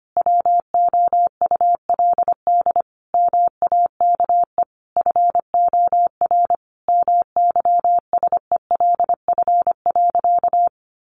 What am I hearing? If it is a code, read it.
Morse code, heard as a keyed tone, about 25 wpm, 700 Hz. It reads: WOULD MAKE FOR MYSELF.
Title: Sentences from Top 1300 Words 25wpm